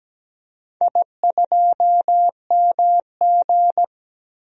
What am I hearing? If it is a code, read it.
Morse code, heard as a keyed tone, about 17 wpm, 700 Hz. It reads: I2MG
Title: DXCC-100 Call Signs - 4 Characters - Repeated 2x - 17wpm